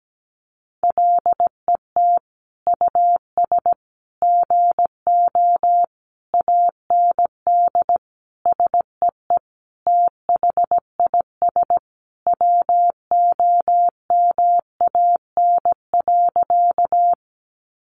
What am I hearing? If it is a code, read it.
Morse code, heard as a keyed tone, about 17 wpm, 700 Hz. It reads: LET US GO AND SEE THIS WOMAN.
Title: Sentences from Top 1100 Words 17wpm